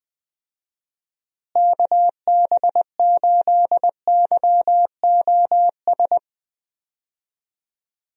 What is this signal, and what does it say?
Morse code, heard as a keyed tone, about 21 wpm, 700 Hz. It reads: KB8YOS
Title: USA Call Signs - 6 Characters - Repeated 2x - 20wpm